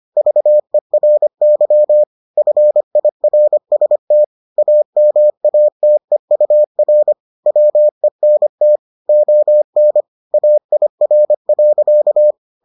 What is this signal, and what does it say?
Morse code, heard as a keyed tone, about 25 wpm, 600 Hz. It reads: VERY FIRST AMATEUR WENT ON AIR.